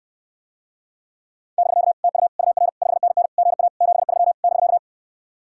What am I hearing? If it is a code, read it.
Morse code, heard as a keyed tone, about 68 wpm, 700 Hz. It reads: <BK> TU UR 5NN GA 73 <BK>